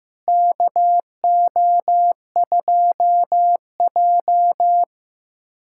Morse code, heard as a keyed tone, about 15 wpm, 700 Hz. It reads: KO2J
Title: USA Callsign DE Callsign - 4 Characters - 15wpm